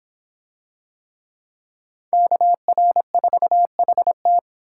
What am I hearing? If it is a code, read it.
Morse code, heard as a keyed tone, about 26 wpm, 700 Hz. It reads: KR4HT